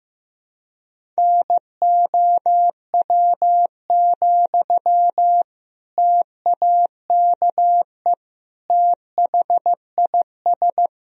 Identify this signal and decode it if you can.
Morse code, heard as a keyed tone, about 15 wpm, 700 Hz. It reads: NOW, TAKE THIS
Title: Sentences from Top 400 Words and Abbreviations 15wpm